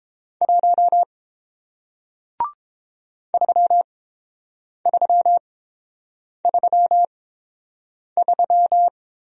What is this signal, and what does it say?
Morse code, heard as a keyed tone, about 30 wpm, spaced as for 7 wpm, 700 Hz. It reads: 1 3333